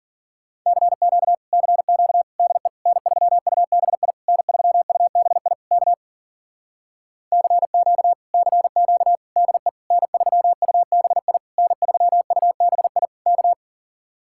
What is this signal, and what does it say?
Morse code, heard as a keyed tone, about 50 wpm, 700 Hz. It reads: CQ CQ DE N3UBI N3UBI K CQ CQ DE N3UBI N3UBI K
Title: Speed-Racing Common QSO Phrases - 40wpm